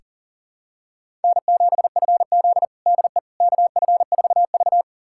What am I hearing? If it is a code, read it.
Morse code, heard as a keyed tone, about 40 wpm, 700 Hz. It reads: N7FZ DE KF4V
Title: USA Callsign DE Callsign - 4 Characters - 40wpm